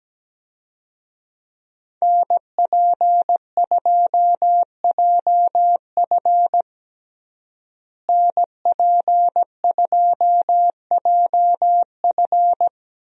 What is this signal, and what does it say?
Morse code, heard as a keyed tone, about 17 wpm, 700 Hz. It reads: NP2JF NP2JF
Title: USA Call Signs - 5 Characters - Repeated 2x - 17wpm